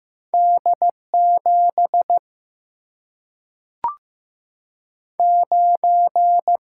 Morse code, heard as a keyed tone, about 15 wpm, 700 Hz. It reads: D 7 9
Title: Single-Letter Single-Number 15wpm